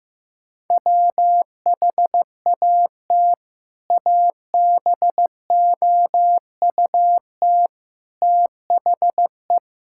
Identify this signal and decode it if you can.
Morse code, heard as a keyed tone, about 16 wpm, 700 Hz. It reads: WHAT ABOUT THE
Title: Sentences from Top 1600 Words 15wpm